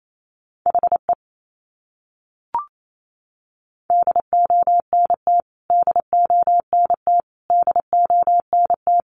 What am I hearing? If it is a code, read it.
Morse code, heard as a keyed tone, about 28 wpm, 700 Hz. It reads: HE DONT DONT DONT
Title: Top 500 Words Repeated 28wpm